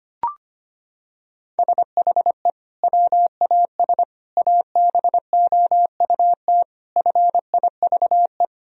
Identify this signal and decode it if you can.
Morse code, heard as a keyed tone, about 25 wpm, 700 Hz. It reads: SHE WAS ABOUT FIVE
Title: Sentences from Top 1000 Words - Review Part 2 - 25wpm